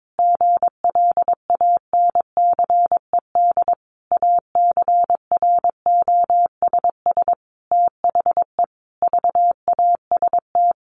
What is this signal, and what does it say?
Morse code, heard as a keyed tone, about 22 wpm, 700 Hz. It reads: GLANCED ACROSS THE VAST
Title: A Sound Of Thunder 22wpm